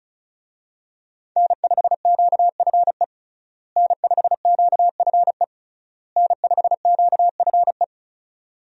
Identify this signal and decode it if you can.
Morse code, heard as a keyed tone, about 34 wpm, 700 Hz. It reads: N5QFE N5QFE N5QFE
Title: USA Call Signs - 5 Characters - Repeated 3x - 35wpm